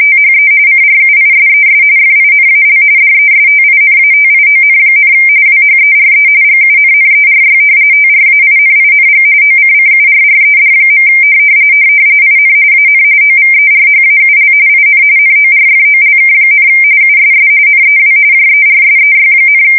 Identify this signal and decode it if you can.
RTTY, about 45.45 baud. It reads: THE CRUISES FINAL DESTINATION, WHERE GUNNERY TRAINING OCCURRED. IN AN OVERHAUL IN NORFOLK, USS NEW JERSEY,
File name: fsk170 / battleshipnj